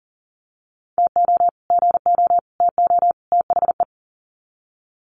Morse code, heard as a keyed tone, about 40 wpm, 700 Hz. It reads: TO GO TO THE
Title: Sets of 4 Words 40wpm